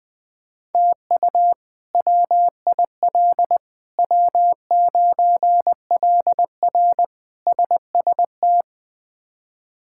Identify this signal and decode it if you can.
Morse code, heard as a keyed tone, about 20 wpm, 700 Hz. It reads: TU WIL W9LR SST
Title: K1USN SST QSO Practice - 20wpm